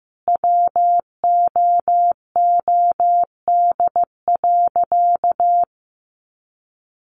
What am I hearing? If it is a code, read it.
Morse code, heard as a keyed tone, about 15 wpm, 700 Hz. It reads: WOOD.
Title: Sentences from Top 1500 Words 15wpm